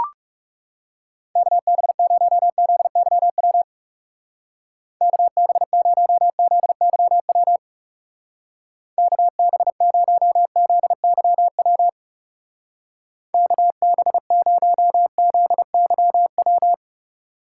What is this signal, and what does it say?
Morse code, heard as a keyed tone, about 39 wpm, 700 Hz. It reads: KB0ZYW KB0ZYW KB0ZYW KB0ZYW